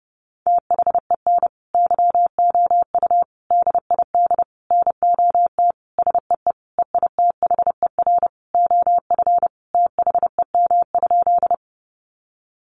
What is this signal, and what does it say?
Morse code, heard as a keyed tone, about 30 wpm, 700 Hz. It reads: THEN YOU DID NOT SEE EITHER OF THEM?
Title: Sentences from Top 600 Words 30wpm